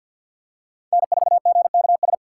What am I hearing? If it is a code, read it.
Morse code, heard as a keyed tone, about 53 wpm, 700 Hz. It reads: N4GKS
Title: USA Call Signs - 5 Characters - 8x Character Spacing - 50wpm